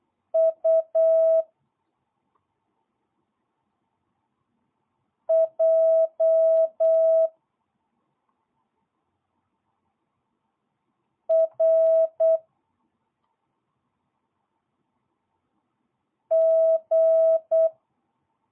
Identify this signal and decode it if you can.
Morse code, heard as a keyed tone, about 8 wpm, 1300 Hz, with noise. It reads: U J R G
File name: jh8cbh / 2022cw